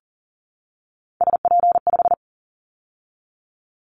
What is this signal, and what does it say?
Morse code, heard as a keyed tone, about 39 wpm, 700 Hz. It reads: SP5
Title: Callsign Prefixes for Top 25 DXCC - 40wpm